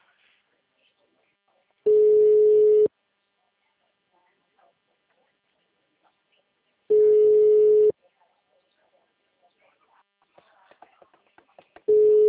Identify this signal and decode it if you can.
Ringback tone (Europe, 425 Hz).
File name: out- -2010- - - .95591